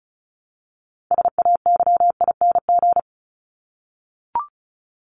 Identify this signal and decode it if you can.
Morse code, heard as a keyed tone, about 34 wpm, 700 Hz. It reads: SAYING
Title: ING Suffix - Top 100 Words with Base Words 35wpm